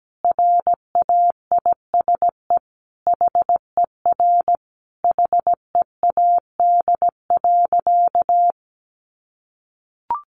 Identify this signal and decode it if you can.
Morse code, heard as a keyed tone, about 17 wpm, 700 Hz. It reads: RAISE HER HEAD.
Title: Sentences from Top 1400 Words 17wpm